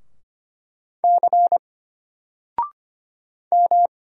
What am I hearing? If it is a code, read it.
Morse code, heard as a keyed tone, about 25 wpm, 700 Hz. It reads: C M
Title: Single Letter-Number 25wpm